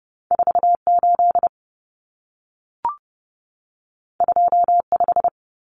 Morse code, heard as a keyed tone, about 30 wpm, 700 Hz. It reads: 48 25
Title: Numbers - 2-Digit 30wpm